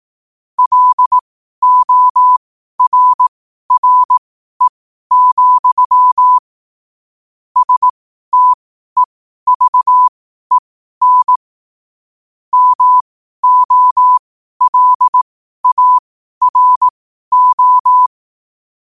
Morse code, heard as a keyed tone, about 18 wpm, spaced as for 12 wpm, 1000 Hz. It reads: LORRE, STEVEN MOLARO
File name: qotdfiles / 12wpm